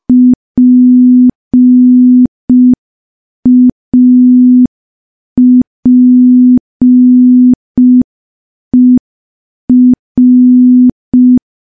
Morse code, heard as a keyed tone, about 5 wpm, 250 Hz. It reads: PAPER
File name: morse / morse-training-05WPM